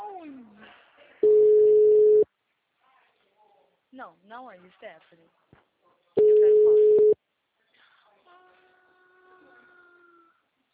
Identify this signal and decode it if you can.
Ringback tone (Europe, 425 Hz), with speech.